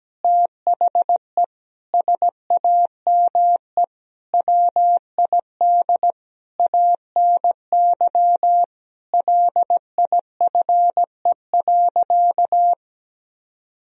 Morse code, heard as a keyed tone, about 17 wpm, 700 Hz. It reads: THE SAME WID ANY LIFE.
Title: Sentences from Top 300 Words and Abbreviations 17wpm